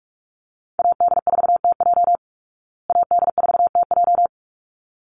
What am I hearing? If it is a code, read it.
Morse code, heard as a keyed tone, about 44 wpm, 700 Hz. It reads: AD4TJ AD4TJ
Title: Call Signs Repeated 3x - 45wpm